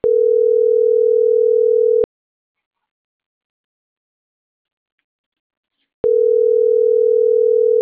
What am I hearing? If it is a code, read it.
Ringback tone (North America), 440 + 480 Hz.